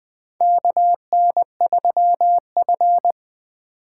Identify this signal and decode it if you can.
Morse code, heard as a keyed tone, about 20 wpm, 700 Hz. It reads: KN3F
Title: USA Call Signs - 4 Characters - 2x Character Spacing - 20wpm